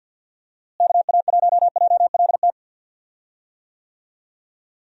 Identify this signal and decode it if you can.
Morse code, heard as a keyed tone, about 51 wpm, 700 Hz. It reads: KA1JLT
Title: USA Call Signs - 6 Characters - Repeated 2x - 50wpm